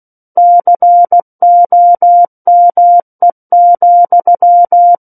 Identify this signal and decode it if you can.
Morse code, heard as a keyed tone, about 16 wpm, 700 Hz. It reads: COME,
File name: 03-Chapter-03 16 WPM